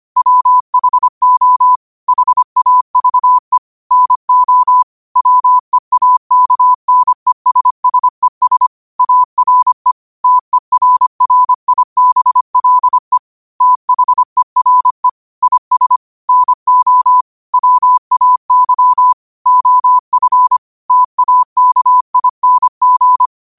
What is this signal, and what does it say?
Morse code, heard as a keyed tone, about 25 wpm, 1000 Hz. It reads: WHO HAVE NO WEAKNESSES ARE TERRIBLE THERE IS NO WAY OF TAKING
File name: qotdfiles / 25wpm